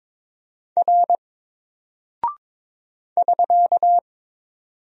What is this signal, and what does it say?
Morse code, heard as a keyed tone, about 22 wpm, 700 Hz. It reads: R <SK>
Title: Single Character (Letter:Number:Punctuation:Prosign) 22wpm